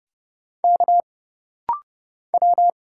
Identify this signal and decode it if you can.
Morse code, heard as a keyed tone, about 30 wpm, 700 Hz. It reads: K W
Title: Single Letter-Number - Rapid-Fire 30wpm